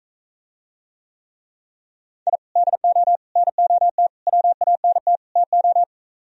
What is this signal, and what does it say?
Morse code, heard as a keyed tone, about 45 wpm, 700 Hz. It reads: I DO NOT WANT TO